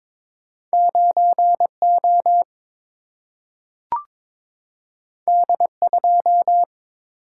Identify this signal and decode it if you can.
Morse code, heard as a keyed tone, about 22 wpm, 700 Hz. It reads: 9 O D 2
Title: Single-Letter Single-Number 22wpm